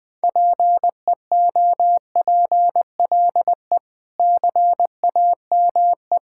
Morse code, heard as a keyed tone, about 20 wpm, 700 Hz. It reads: PEOPLE CAME
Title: Sentences with 5-Word Phrases - 20wpm